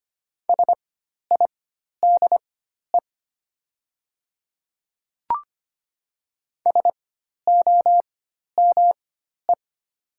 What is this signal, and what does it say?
Morse code, heard as a keyed tone, about 25 wpm, spaced as for 12 wpm, 700 Hz. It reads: SIDE SOME
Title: 4x Character Spacing - Top 500 Words - 25wpm